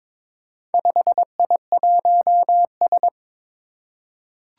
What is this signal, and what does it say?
Morse code, heard as a keyed tone, about 22 wpm, 700 Hz. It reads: 5I1S
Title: Four Challenging Letter-Number Combinations 22wpm